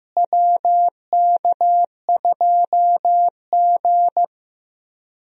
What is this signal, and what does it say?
Morse code, heard as a keyed tone, about 15 wpm, 700 Hz. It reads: WK2G
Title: USA Call Signs - 4 Characters - Repeated 3x - 15wpm